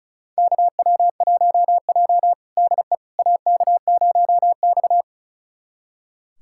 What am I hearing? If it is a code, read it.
Morse code, heard as a keyed tone, about 36 wpm, 700 Hz. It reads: KW1J DE AK0X
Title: USA Callsign DE Callsign - 4 Characters - 35wpm